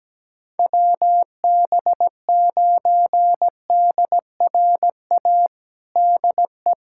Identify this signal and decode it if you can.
Morse code, heard as a keyed tone, about 17 wpm, 700 Hz. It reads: WB9DRA DE
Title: USA Callsign DE Callsign - 6 Characters - 17wpm